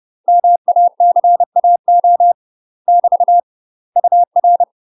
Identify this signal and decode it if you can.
Morse code, heard as a keyed tone, about 30 wpm, 700 Hz. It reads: MACAO <BT> UR
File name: UK MASTER QSOs TRAINING / 30WPM / UK MASTER QSOs 0019